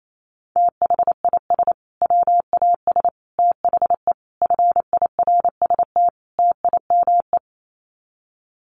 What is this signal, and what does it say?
Morse code, heard as a keyed tone, about 28 wpm, 700 Hz. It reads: THIS WAS THE FIRST TIME
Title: Sets of 5 Words Encore 28wpm